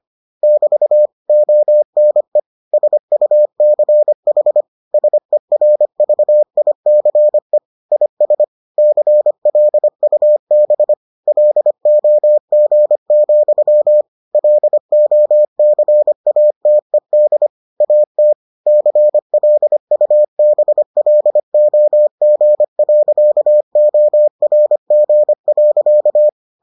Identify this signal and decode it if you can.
Morse code, heard as a keyed tone, about 25 wpm, 600 Hz. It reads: <BT> ONE SUCH SERVICE IS CLUB LOG, LOCATED AT CLUBLOG.ORG.